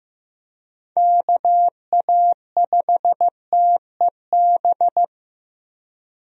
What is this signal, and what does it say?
Morse code, heard as a keyed tone, about 15 wpm, 700 Hz. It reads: KA5TEB
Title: USA Call Signs - 6 Characters - Repeated 2x - 15wpm